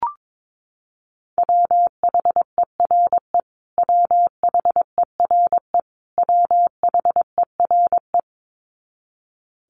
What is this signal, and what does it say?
Morse code, heard as a keyed tone, about 22 wpm, 700 Hz. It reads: WHERE WHERE WHERE
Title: Top 300 Words Repeated 22wpm